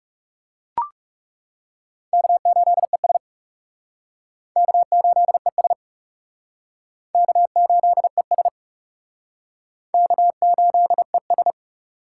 Morse code, heard as a keyed tone, about 40 wpm, 700 Hz. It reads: K8ES K8ES K8ES K8ES